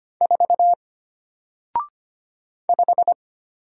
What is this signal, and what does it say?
Morse code, heard as a keyed tone, about 25 wpm, 700 Hz. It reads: 4 5
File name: Single-Letter-Number-25wpm